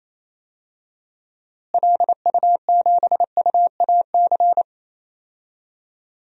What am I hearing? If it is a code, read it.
Morse code, heard as a keyed tone, about 28 wpm, 700 Hz. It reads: LU7UAC
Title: DXCC-100 Call Signs - 6 Characters - Repeated 3x - 28wpm